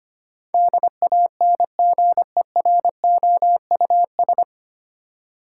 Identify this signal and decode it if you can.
Morse code, heard as a keyed tone, about 25 wpm, 690 Hz. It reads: DANGEROUS
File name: practice / 25wpm / 9 Letter Words - First 3 Letters Given - 25wpm